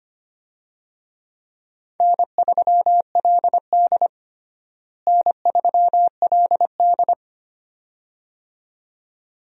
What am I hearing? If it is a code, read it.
Morse code, heard as a keyed tone, about 25 wpm, 700 Hz. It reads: N3LD N3LD
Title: USA Call Signs - 4 Characters - Repeated 2x - 25wpm